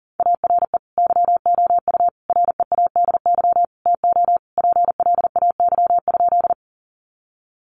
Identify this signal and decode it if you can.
Morse code, heard as a keyed tone, about 40 wpm, 700 Hz. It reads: ARE YOU READY TO PLAY?
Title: Sentences from Top 400 Words 40wpm